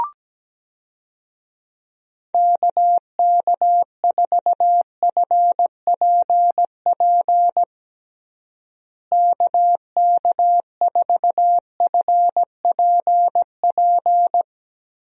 Morse code, heard as a keyed tone, about 17 wpm, 700 Hz. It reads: KK4FPP KK4FPP
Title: USA Call Signs - 6 Characters - Repeated 3x - 17wpm